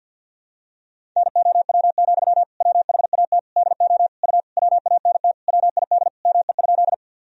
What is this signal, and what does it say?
Morse code, heard as a keyed tone, about 49 wpm, 700 Hz. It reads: NOW, WHAT DO U WANT WID ME?
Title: Sentences from Top 200 Words and Abbreviations 50wpm